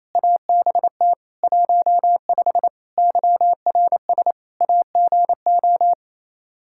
Morse code, heard as a keyed tone, about 28 wpm, 700 Hz. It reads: ABT 15 YRS AGO
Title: K1USN SST QSO Practice 28wpm